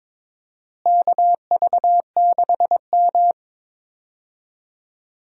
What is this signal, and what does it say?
Morse code, heard as a keyed tone, about 21 wpm, 700 Hz. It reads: KV6M
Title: USA Call Signs - 4 Characters - Repeated 2x - 22wpm